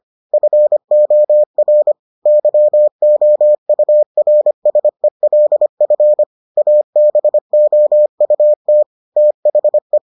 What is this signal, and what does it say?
Morse code, heard as a keyed tone, about 25 wpm, 600 Hz. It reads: FOR YOURSELF ABOUT THE